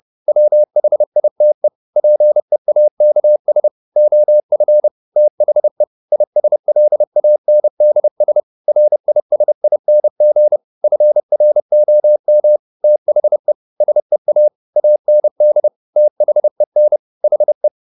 Morse code, heard as a keyed tone, about 30 wpm, 600 Hz. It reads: WHITE PEAKS OF THE ISLANDS RISING FROM THE SEA AND THEN HE
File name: the old man and the sea / the old man and the sea 1 30wpm